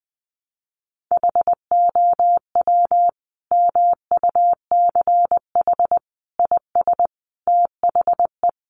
Morse code, heard as a keyed tone, about 20 wpm, 700 Hz. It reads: HOW MUCH IS THE
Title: Sentences from Top 1200 Words 20wpm